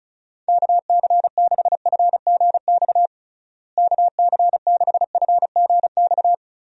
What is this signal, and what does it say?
Morse code, heard as a keyed tone, about 36 wpm, 700 Hz. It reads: KC6FGX KC6FGX
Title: USA Call Signs - 6 Characters - Repeated 3x - 35wpm